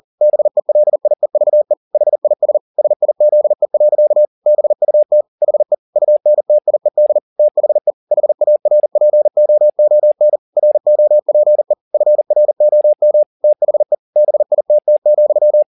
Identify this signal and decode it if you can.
Morse code, heard as a keyed tone, about 40 wpm, 600 Hz. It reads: BELIEVE HIS SIZE. BUT HE UNTIED THE HARPOON ROPE FROM THE BITT,
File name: the old man and the sea 2 40wpm